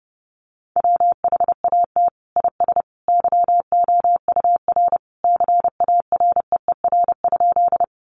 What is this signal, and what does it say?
Morse code, heard as a keyed tone, about 30 wpm, 700 Hz. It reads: WHAT IS YOUR CAREER?
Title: Sentences from Top 1000 Words 30wpm